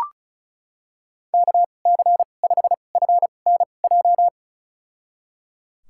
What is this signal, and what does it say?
Morse code, heard as a keyed tone, about 34 wpm, spaced as for 25 wpm, 700 Hz. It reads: KC5FNJ
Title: USA Call Signs - 6 Characters - 2x Character Spacing - 35wpm